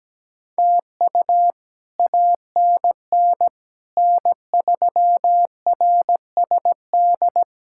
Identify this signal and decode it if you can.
Morse code, heard as a keyed tone, about 17 wpm, 700 Hz. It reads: TU ANN N3RSD
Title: K1USN SST QSO Practice - 17wpm